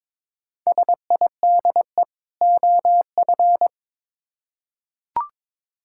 Morse code, heard as a keyed tone, about 22 wpm, 700 Hz. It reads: SIDE OF
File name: Build A Phrase - 2-Word Phrases - 22wpm